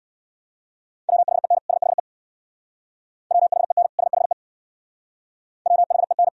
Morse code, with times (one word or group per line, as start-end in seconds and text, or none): 1.1-1.6 WHEN
1.7-2.0 SHE
3.3-3.9 WHEN
4.0-4.3 SHE
5.7-6.3 WHEN